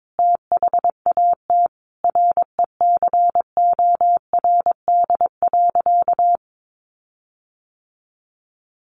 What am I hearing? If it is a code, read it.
Morse code, heard as a keyed tone, about 22 wpm, 700 Hz. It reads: THAT RECORD.
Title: Sentences from Top 1400 Words 22wpm